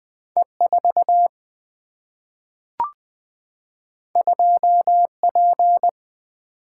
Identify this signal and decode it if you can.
Morse code, heard as a keyed tone, about 20 wpm, 700 Hz. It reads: E 4 2 P
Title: Two Letter-Number Combinations - 20wpm